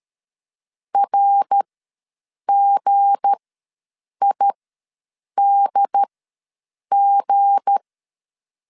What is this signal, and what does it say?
Morse code, heard as a keyed tone, about 13 wpm, spaced as for 7 wpm, 800 Hz. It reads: RGIDG
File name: LEZIONI / 7 GF9Z / 7 G ws 12wpm 3140